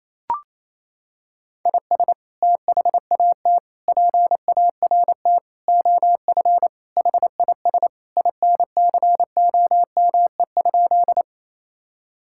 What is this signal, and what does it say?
Morse code, heard as a keyed tone, about 27 wpm, 700 Hz. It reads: IS THAT PART OF HIS INCOME?
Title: Sentences from Top 1000 Words - Review Part 5 - 28wpm